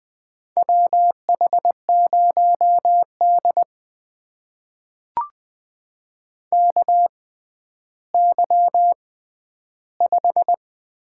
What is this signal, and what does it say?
Morse code, heard as a keyed tone, about 20 wpm, spaced as for 11 wpm, 700 Hz. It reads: WH0D KY5
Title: USA Call Signs - 4 Characters - 6x Character Spacing - 20wpm